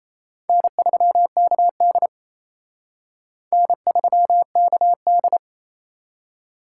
Morse code, heard as a keyed tone, about 34 wpm, 700 Hz. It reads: N3KD N3KD
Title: Speed Racing - USA 1x2 Call Signs 28wpm